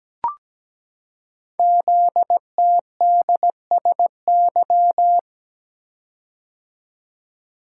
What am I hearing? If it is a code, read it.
Morse code, heard as a keyed tone, about 17 wpm, 700 Hz. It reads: ZTDSY
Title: Code Groups - 5 Letters - 17wpm